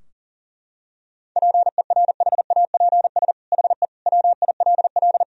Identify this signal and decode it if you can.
Morse code, heard as a keyed tone, about 40 wpm, 700 Hz. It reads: PERHAPS HE WILL
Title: Sentences from Top 1000 Words - Review Part 4 - 40wpm